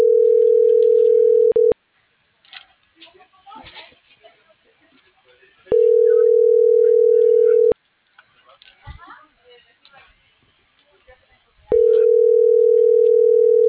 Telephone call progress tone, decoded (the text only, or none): ringback tone